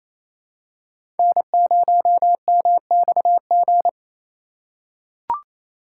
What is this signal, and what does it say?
Morse code, heard as a keyed tone, about 28 wpm, 700 Hz. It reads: N0MXG
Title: USA Call Signs - 4x Character Spacing - 28wpm